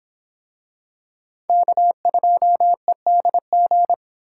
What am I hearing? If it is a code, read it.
Morse code, heard as a keyed tone, about 26 wpm, 700 Hz. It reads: K2EDG